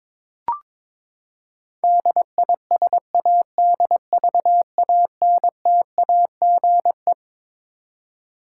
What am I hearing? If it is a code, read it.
Morse code, heard as a keyed tone, about 22 wpm, 700 Hz. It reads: DISADVANTAGE
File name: Twelve Letter Words - 22wpm